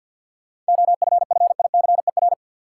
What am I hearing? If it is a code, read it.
Morse code, heard as a keyed tone, about 50 wpm, 700 Hz. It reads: OFFICER
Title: Sentences from Top 1200 Words - New Word Drill - 50wpm